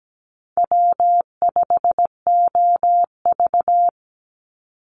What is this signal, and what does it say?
Morse code, heard as a keyed tone, about 17 wpm, 700 Hz. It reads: W5OV
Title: Easy Call Signs - 17wpm